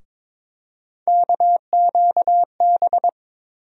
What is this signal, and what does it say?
Morse code, heard as a keyed tone, about 22 wpm, 700 Hz. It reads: KQB
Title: USA Call Sign Postfixes - 3 Letters 22wpm